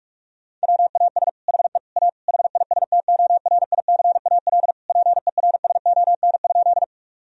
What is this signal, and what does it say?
Morse code, heard as a keyed tone, about 46 wpm, 700 Hz. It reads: WAS HE A HISTORICAL PERSON?
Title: Sentences from Top 1000 Words - Review Part 4 - 45wpm